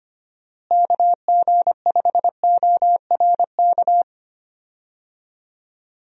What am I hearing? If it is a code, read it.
Morse code, heard as a keyed tone, about 25 wpm, 700 Hz. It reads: KG5ORK
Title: USA Call Signs - 6 Characters - Repeated 2x - 25wpm